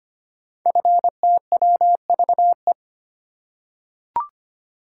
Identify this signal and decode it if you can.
Morse code, heard as a keyed tone, about 25 wpm, 700 Hz. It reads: FTWVE
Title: Code Groups - 5 Letters - 25wpm